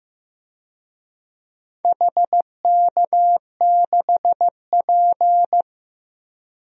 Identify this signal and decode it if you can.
Morse code, heard as a keyed tone, about 15 wpm, 700 Hz. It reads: HK6P
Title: DXCC-100 Call Signs - 4 Characters - Repeated 3x - 15wpm